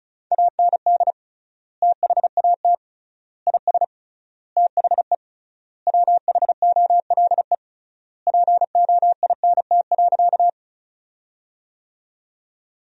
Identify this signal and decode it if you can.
Morse code, heard as a keyed tone, about 36 wpm, 700 Hz. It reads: AND THAT IS THE WHOLE POINT.
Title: 3x Word-Spacing - Sentences from Top 400 Words - 35wpm